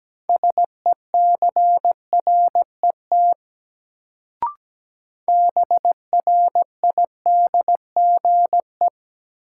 Morse code, heard as a keyed tone, about 17 wpm, 700 Hz. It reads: SECRET BRIDGE
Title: Sentences from Top 1200 Words - New Word Drill - 17wpm